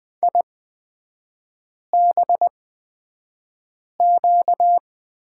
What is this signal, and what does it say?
Morse code, heard as a keyed tone, about 20 wpm, spaced as for 5 wpm, 700 Hz. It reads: IBQ
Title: 236 Lesson 38e - Letter q - Three Characters 20wpm